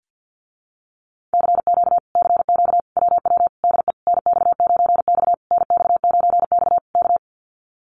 Morse code, heard as a keyed tone, about 50 wpm, 700 Hz. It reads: CQ CQ WW DE NX9X NX9X K
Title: Common QSO Phrases Encore 50wpm